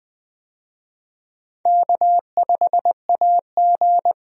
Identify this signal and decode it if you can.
Morse code, heard as a keyed tone, about 20 wpm, 700 Hz. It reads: K5AG
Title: USA Call Signs - 4 Characters - Repeated 2x - 20wpm